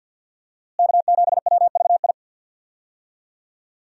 Morse code, heard as a keyed tone, about 49 wpm, 700 Hz. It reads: K7WVI
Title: USA Call Signs - 5 Characters - Repeated 2x - 50wpm